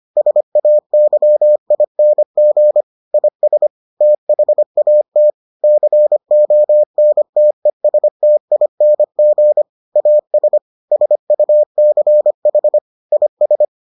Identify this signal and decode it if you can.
Morse code, heard as a keyed tone, about 25 wpm, 600 Hz. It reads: SAYING IS THAT CONTESTING AS SUCH IS